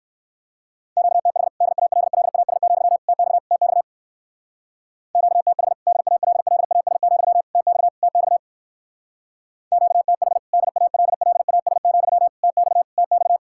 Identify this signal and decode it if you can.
Morse code, heard as a keyed tone, about 59 wpm, 700 Hz. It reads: QTH DALLAS, TX TX QTH DALLAS, TX TX QTH DALLAS, TX TX